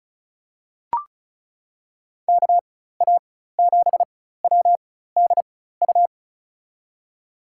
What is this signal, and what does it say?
Morse code, heard as a keyed tone, about 36 wpm, spaced as for 16 wpm, 700 Hz. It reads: KA7WDU
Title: USA Call Signs - 6 Characters - 4x Character Spacing - 35wpm